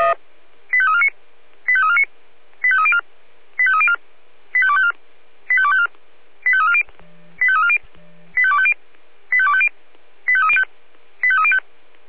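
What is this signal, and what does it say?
Five-tone selective call (ZVEI), heard as quick runs of five tones, tones of about 70 ms. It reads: 85349 85349 85304 85304 85254 85254 85340 85340 85250 85250 85305 85305
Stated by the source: Five-tone ZVEI selective call received on an SDR.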